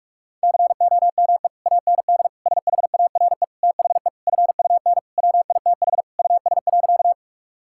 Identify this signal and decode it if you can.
Morse code, heard as a keyed tone, about 46 wpm, 700 Hz. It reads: COME AND SHARE THE FUN WITH US.
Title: Sentences from Top 1000 Words - Review Part 6 - 45wpm